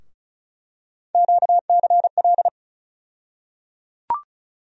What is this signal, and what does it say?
Morse code, heard as a keyed tone, about 36 wpm, 700 Hz. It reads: QCL
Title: USA Call Sign Postfixes - 3 Letters 35wpm